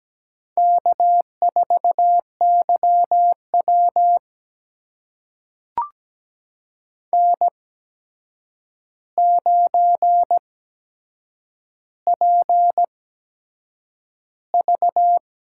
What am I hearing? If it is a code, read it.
Morse code, heard as a keyed tone, about 17 wpm, spaced as for 7 wpm, 700 Hz. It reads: K4YW N9PV
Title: USA Call Signs - 4 Characters - 8x Character Spacing - 17wpm